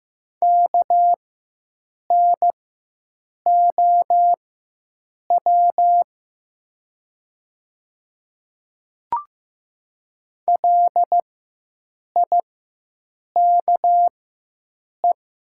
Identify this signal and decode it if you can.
Morse code, heard as a keyed tone, about 15 wpm, spaced as for 7 wpm, 700 Hz. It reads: KNOW LIKE
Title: 4x Character Spacing - Top 100 Words - 15wpm